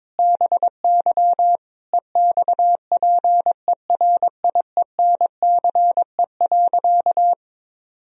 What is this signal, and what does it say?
Morse code, heard as a keyed tone, about 22 wpm, 700 Hz. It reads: BY EXPERIENCE.
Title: Sentences from Top 1000 Words - Review Part 3 - 22wpm